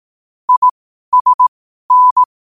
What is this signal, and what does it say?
Morse code, heard as a keyed tone, about 18 wpm, spaced as for 12 wpm, 1000 Hz. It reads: ISN
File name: qotdfiles / 12wpm